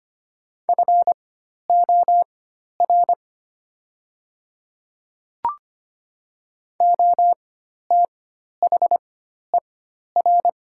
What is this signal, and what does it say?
Morse code, heard as a keyed tone, about 25 wpm, spaced as for 12 wpm, 700 Hz. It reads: FOR OTHER
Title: 4x Character Spacing - Top 100 Words - 25wpm